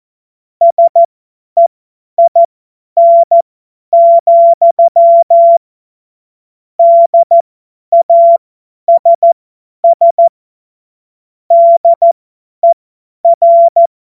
Morse code, heard as a keyed tone, about 14 wpm, spaced as for 10 wpm, 680 Hz. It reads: SEIN, DASS DER
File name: Morsekurs / Witze / Witz 06 Februar 2024 50 F70